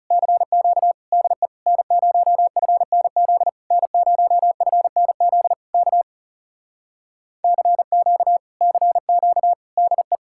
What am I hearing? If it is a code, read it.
Morse code, heard as a keyed tone, about 37 wpm, 700 Hz. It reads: CQ DE N0FNZ N0FNZ K CQ CQ DE
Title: Speed-Racing Common QSO Phrases - 30wpm